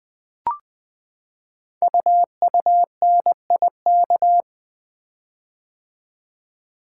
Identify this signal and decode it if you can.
Morse code, heard as a keyed tone, about 20 wpm, 700 Hz. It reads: UUNIK
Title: Code Groups - 5 Letters - 20wpm